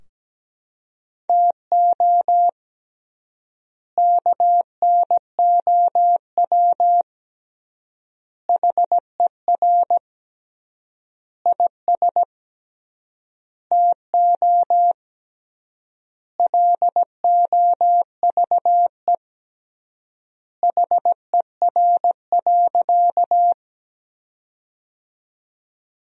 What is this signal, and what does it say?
Morse code, heard as a keyed tone, about 17 wpm, 700 Hz. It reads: TO KNOW HER IS TO LOVE HER.
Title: 3x Word-Spacing - Sentences from Top 400 Words - 17wpm